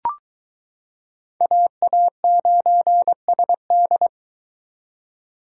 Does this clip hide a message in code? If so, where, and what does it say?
Yes: Morse code from 1.4 to 4.1 s, heard as a keyed tone, about 22 wpm, 700 Hz. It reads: AA9SD